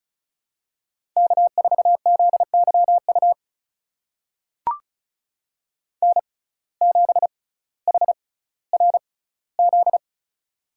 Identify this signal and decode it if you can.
Morse code, heard as a keyed tone, about 34 wpm, spaced as for 18 wpm, 700 Hz. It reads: K4ZYU N7HRZ
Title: USA Call Signs - 5 Characters - 6x Character Spacing - 35wpm